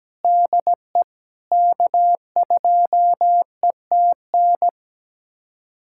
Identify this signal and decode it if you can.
Morse code, heard as a keyed tone, about 17 wpm, 700 Hz. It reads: DE K2ETN
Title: USA Callsign DE Callsign - 5 Characters - 17wpm